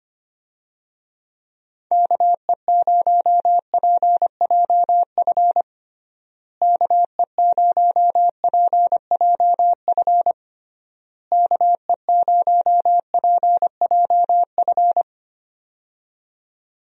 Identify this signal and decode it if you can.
Morse code, heard as a keyed tone, about 25 wpm, 700 Hz. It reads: KE0PJF KE0PJF KE0PJF
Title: USA Call Signs - 6 Characters - Repeated 3x - 25wpm